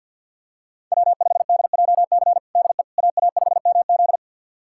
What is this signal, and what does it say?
Morse code, heard as a keyed tone, about 50 wpm, 700 Hz. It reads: W5DJC DE AA5MZ
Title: USA Callsign DE Callsign - 5 Characters - 50wpm